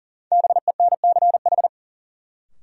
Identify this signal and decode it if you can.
Morse code, heard as a keyed tone, about 40 wpm, 700 Hz. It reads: BENCH
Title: Sentences from Top 1400 Words - New Word Drill Alt - 40wpm